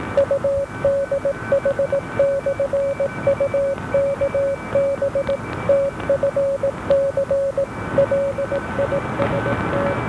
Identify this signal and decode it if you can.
Morse code, heard as a keyed tone, about 18 wpm, 560 Hz, with noise. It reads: UDH/UKB TFC LIST